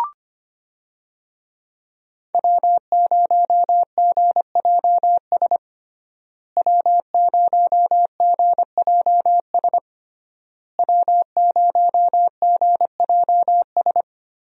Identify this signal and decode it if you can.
Morse code, heard as a keyed tone, about 25 wpm, 700 Hz. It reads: W0GJS W0GJS W0GJS
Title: USA Call Signs - 5 Characters - Repeated 3x - 25wpm